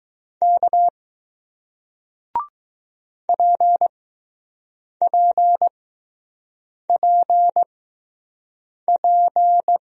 Morse code, heard as a keyed tone, about 20 wpm, spaced as for 6 wpm, 700 Hz. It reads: K PPPP